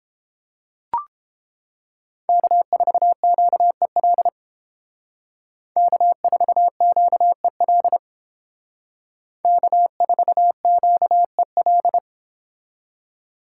Morse code, heard as a keyed tone, about 30 wpm, 700 Hz. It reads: K4QEL K4QEL K4QEL